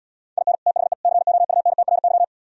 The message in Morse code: IN THE COUNTRY

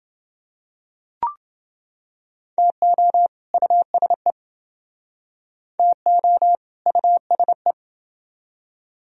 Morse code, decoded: TO USE TO USE

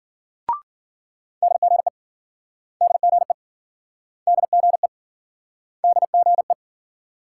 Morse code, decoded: DGE DGE DGE DGE

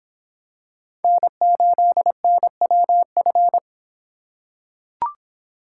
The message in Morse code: N8NWF

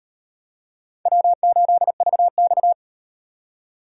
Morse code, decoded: W8VX

